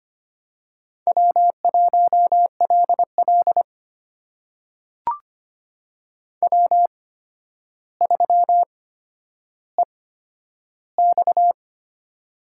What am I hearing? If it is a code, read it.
Morse code, heard as a keyed tone, about 25 wpm, spaced as for 11 wpm, 700 Hz. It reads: W1LL W3EX
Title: USA Call Signs - 4 Characters - 8x Character Spacing - 25wpm